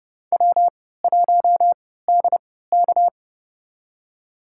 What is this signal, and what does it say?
Morse code, heard as a keyed tone, about 32 wpm, spaced as for 17 wpm, 700 Hz. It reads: W1DK
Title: USA Call Signs - 3x Character Spacing - 30wpm